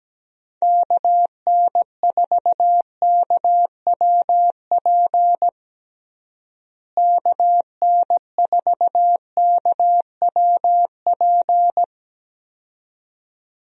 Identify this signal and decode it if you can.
Morse code, heard as a keyed tone, about 17 wpm, 700 Hz. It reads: KN4KWP KN4KWP
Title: USA Call Signs - 6 Characters - Repeated 2x - 17wpm